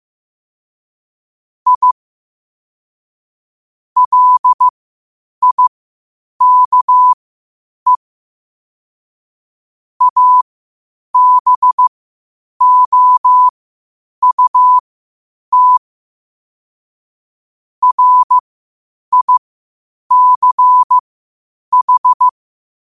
Morse code, heard as a keyed tone, about 15 wpm, spaced as for 8 wpm, 1000 Hz. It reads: I LIKE ABOUT RICH
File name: qotdfiles / 8wpm